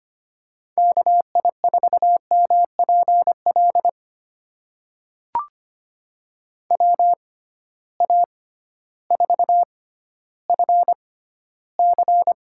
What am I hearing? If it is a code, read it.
Morse code, heard as a keyed tone, about 25 wpm, spaced as for 13 wpm, 700 Hz. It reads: KI4MPL WA4FC
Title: USA Call Signs - 6 Characters - 6x Character Spacing - 25wpm